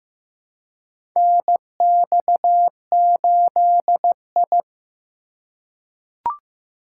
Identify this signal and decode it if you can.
Morse code, heard as a keyed tone, about 15 wpm, 700 Hz. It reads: NX8I
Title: USA Call Signs - 4 Characters - 2x Character Spacing - 15wpm